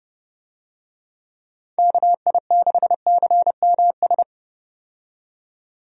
Morse code, heard as a keyed tone, about 31 wpm, 700 Hz. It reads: KI6CMS
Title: USA Call Signs - 6 Characters - 30wpm